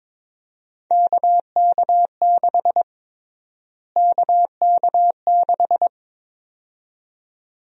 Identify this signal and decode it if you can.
Morse code, heard as a keyed tone, about 22 wpm, 700 Hz. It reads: KK6 KK6
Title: USA Call Sign Prefixes - Repeated 2x - 22wpm